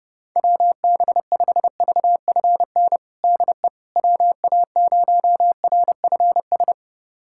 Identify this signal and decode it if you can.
Morse code, heard as a keyed tone, about 32 wpm, 700 Hz. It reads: WB5VFN DE WA0RFS
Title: USA Callsign DE Callsign - 6 Characters - 30wpm